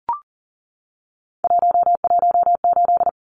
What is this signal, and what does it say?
Morse code, heard as a keyed tone, about 38 wpm, 700 Hz. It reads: 118
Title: Numbers - 3-Digit 40wpm